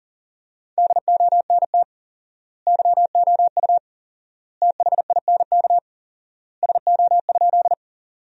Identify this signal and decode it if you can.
Morse code, heard as a keyed tone, about 40 wpm, 700 Hz. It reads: DONT YOU THINK SO?
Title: 4x Word-Spacing - Sentences from Top 300 Words - 40wpm